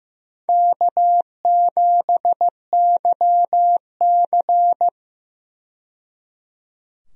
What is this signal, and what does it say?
Morse code, heard as a keyed tone, about 15 wpm, 700 Hz. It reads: K7YC
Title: USA Call Signs - 4 Characters - 15wpm